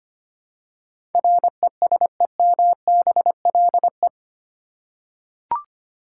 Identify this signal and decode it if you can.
Morse code, heard as a keyed tone, about 25 wpm, 700 Hz. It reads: RESEMBLE
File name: Eight Letter Words - 25wpm